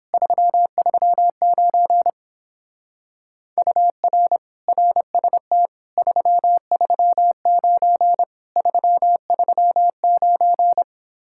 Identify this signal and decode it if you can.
Morse code, heard as a keyed tone, about 29 wpm, 700 Hz. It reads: 339 UR RST 339 339